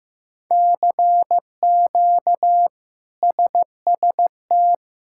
Morse code, heard as a keyed tone, about 15 wpm, 700 Hz. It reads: CQ SST
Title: K1USN SST QSO Practice - 15wpm